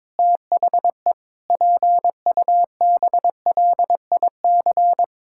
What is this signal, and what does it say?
Morse code, heard as a keyed tone, about 22 wpm, 700 Hz. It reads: THE PUBLIC
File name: practice / 22wpm / Sets of 7 Words - 22wpm